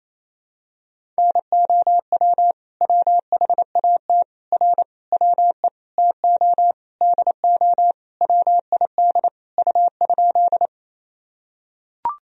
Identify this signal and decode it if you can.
Morse code, heard as a keyed tone, about 28 wpm, 700 Hz. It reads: NOW WHAT R WE TO DO WID U?
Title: Sentences from Top 100 Words and Abbreviations 28wpm